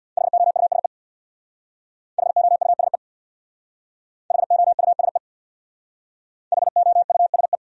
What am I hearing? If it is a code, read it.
Morse code, heard as a keyed tone, about 61 wpm, 700 Hz. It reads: HOUSE HOUSE HOUSE HOUSE
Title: Speed-Racing Top 300 Words 50wpm